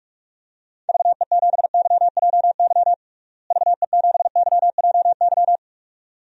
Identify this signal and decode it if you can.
Morse code, heard as a keyed tone, about 46 wpm, 700 Hz. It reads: VE7YJY VE7YJY
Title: DXCC-100 Call Signs - 6 Characters - Repeated 2x - 45wpm